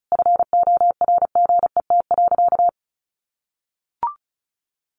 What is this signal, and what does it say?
Morse code, heard as a keyed tone, about 34 wpm, 700 Hz. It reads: FORGET.
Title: Sentences from Top 1500 Words 35wpm